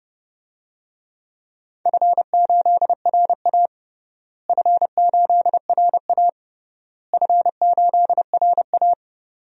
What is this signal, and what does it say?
Morse code, heard as a keyed tone, about 30 wpm, 700 Hz. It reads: F8RA F8RA F8RA
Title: DXCC-100 Call Signs - 4 Characters - Repeated 3x - 30wpm